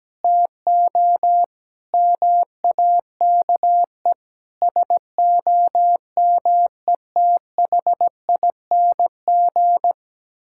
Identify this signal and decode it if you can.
Morse code, heard as a keyed tone, about 17 wpm, 700 Hz. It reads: TO MAKE SOMETHING
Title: Sentences from Top 1000 Words - Review Part 5 - 17wpm